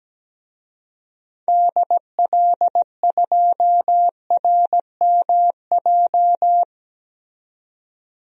Morse code, heard as a keyed tone, about 17 wpm, 700 Hz. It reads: DL2RMJ
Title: DXCC-100 Call Signs - 6 Characters - 17wpm